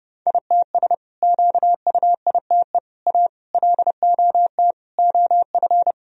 Morse code, heard as a keyed tone, about 30 wpm, 700 Hz. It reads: ITS QUITE A LOT OF
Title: Sentences from Top 1000 Words - Review Part 3 - 30wpm